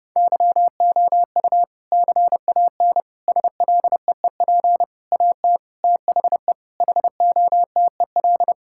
Morse code, heard as a keyed tone, about 30 wpm, 700 Hz. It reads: YOU CAN SLEEP AT THE HOTEL
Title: Sentences from Top 1600 Words 30wpm